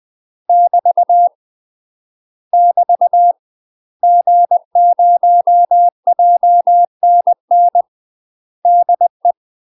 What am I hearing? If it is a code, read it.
Morse code, heard as a keyed tone, about 20 wpm, 700 Hz. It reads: <BT> <BT> G0JNN DE
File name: UK MASTER QSOs TRAINING / 20WPM / UK MASTER QSOs 0072